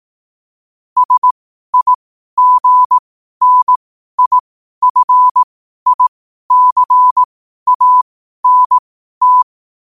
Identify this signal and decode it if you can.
Morse code, heard as a keyed tone, about 18 wpm, spaced as for 12 wpm, 1000 Hz. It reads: SIGNIFICANT
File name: qotdfiles / 12wpm